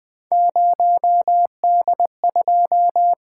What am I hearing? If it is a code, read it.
Morse code, heard as a keyed tone, about 20 wpm, 700 Hz. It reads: 0D2
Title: Three Letter-Number Combinations - 20wpm